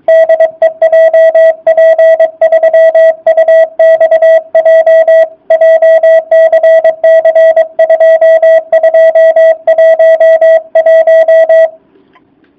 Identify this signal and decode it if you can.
Morse code, heard as a keyed tone, about 22 wpm, 660 Hz. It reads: DE JP3UXJ JCC2211
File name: jh8cbh / 2019cw / sp24- w